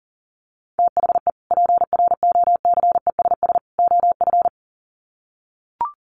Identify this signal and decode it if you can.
Morse code, heard as a keyed tone, about 40 wpm, 700 Hz. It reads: THE PROCESS OF
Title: Sets of 3 Words 40wpm